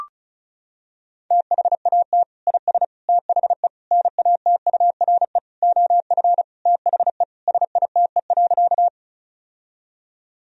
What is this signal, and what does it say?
Morse code, heard as a keyed tone, about 36 wpm, 700 Hz. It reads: THAT IS THE NATURE OF THE SITE.
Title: Sentences from Top 1000 Words - Review Part 4 - 35wpm